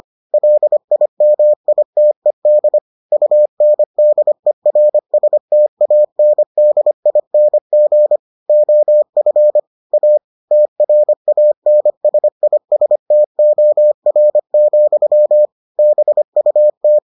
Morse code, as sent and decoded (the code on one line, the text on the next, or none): .-.. .. -- .. - . -.. / ..- -. -.. .-. ... - .- -. -.. .. -. --. / --- ..-. / .- / - .-. .- -. ... .. ... - --- .-. --..-- / -... ..- -
LIMITED UNDERSTANDING OF A TRANSISTOR, BUT